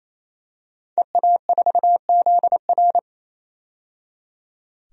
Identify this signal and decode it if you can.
Morse code, heard as a keyed tone, about 28 wpm, 700 Hz. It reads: EA4ZR
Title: DXCC-100 Call Signs - 5 Characters - 28wpm